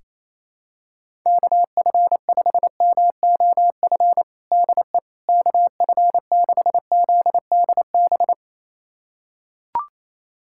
Morse code, heard as a keyed tone, about 28 wpm, 700 Hz. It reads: KF5MOF DE KF6ZDB
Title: USA Callsign DE Callsign - 6 Characters - 28wpm